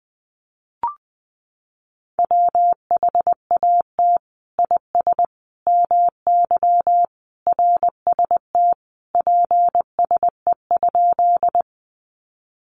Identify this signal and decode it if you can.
Morse code, heard as a keyed tone, about 20 wpm, 700 Hz. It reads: WHAT IS MY RST PSE?
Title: Common QSO Phrases 20wpm